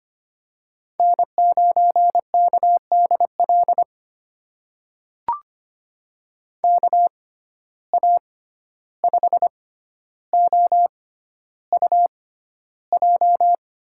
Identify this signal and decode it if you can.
Morse code, heard as a keyed tone, about 25 wpm, spaced as for 12 wpm, 700 Hz. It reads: N9KDL KA5OUJ
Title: USA Call Signs - 6x Character Spacing - 25wpm